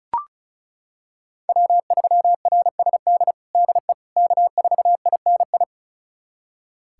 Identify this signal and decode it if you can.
Morse code, heard as a keyed tone, about 36 wpm, 700 Hz. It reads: W3RSD DE K4INI
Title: USA Callsign DE Callsign - 5 Characters - 35wpm